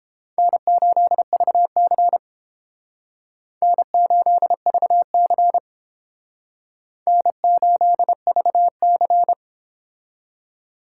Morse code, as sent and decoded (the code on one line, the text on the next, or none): -. ---.. ...- -.-. / -. ---.. ...- -.-. / -. ---.. ...- -.-.
N8VC N8VC N8VC